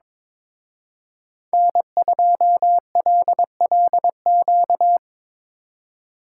Morse code, heard as a keyed tone, about 24 wpm, 700 Hz. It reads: N2LLQ
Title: Speed Racing - USA 1x3 Call Signs 22wpm